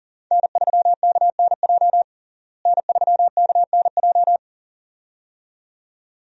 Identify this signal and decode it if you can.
Morse code, heard as a keyed tone, about 37 wpm, 700 Hz. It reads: N3KNJ N3KNJ
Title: USA Call Signs - 5 Characters - Repeated 2x - 40wpm